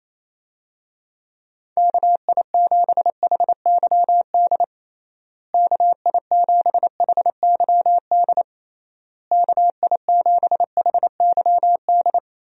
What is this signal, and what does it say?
Morse code, heard as a keyed tone, about 28 wpm, 700 Hz. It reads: KI7HYD KI7HYD KI7HYD
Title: USA Call Signs - 6 Characters - Repeated 3x - 28wpm